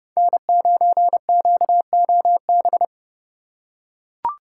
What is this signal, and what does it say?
Morse code, heard as a keyed tone, about 30 wpm, 700 Hz. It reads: N9QOB
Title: USA Call Signs - 5 Characters - 4x Character Spacing - 30wpm